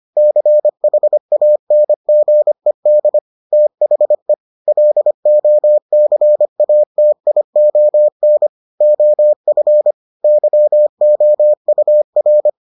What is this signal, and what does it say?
Morse code, heard as a keyed tone, about 25 wpm, 600 Hz. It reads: CHANGED THE LOCATION OF YOUR